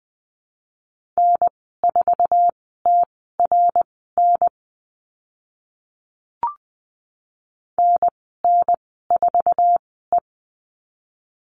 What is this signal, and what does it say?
Morse code, heard as a keyed tone, about 20 wpm, spaced as for 15 wpm, 700 Hz. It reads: N4TRN NN4E
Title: 062 Lesson 11g - Letter h - Call Signs 20wpm